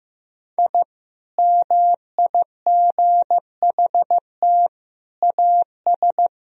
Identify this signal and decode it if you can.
Morse code, heard as a keyed tone, about 16 wpm, 700 Hz. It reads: I MIGHT AS
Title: Sentences from Top 1000 Words - Review Part 2 - 15wpm